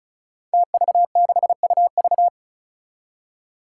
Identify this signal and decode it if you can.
Morse code, heard as a keyed tone, about 36 wpm, 700 Hz. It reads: TV6UV
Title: Five Challenging Letter-Number Combinations 35wpm